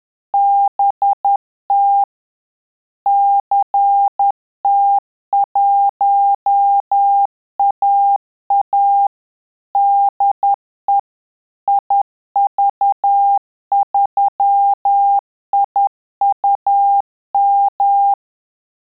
Morse code, heard as a keyed tone, about 10 wpm, 800 Hz. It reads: BT CT1AA DE IV3IUM
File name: Telegraphy-group / QSO CW 50 CPM